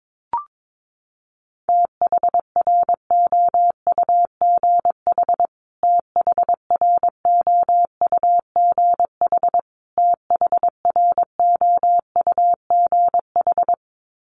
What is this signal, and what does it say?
Morse code, heard as a keyed tone, about 22 wpm, 700 Hz. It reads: THROUGH THROUGH THROUGH
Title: Top 500 Words Repeated 22wpm